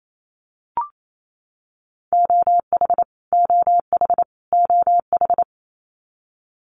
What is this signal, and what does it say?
Morse code, heard as a keyed tone, about 28 wpm, 700 Hz. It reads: OH OH OH
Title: Top 500 Words Repeated 28wpm